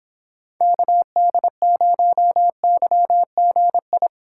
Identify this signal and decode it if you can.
Morse code, heard as a keyed tone, about 23 wpm, 700 Hz. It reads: KD0YGI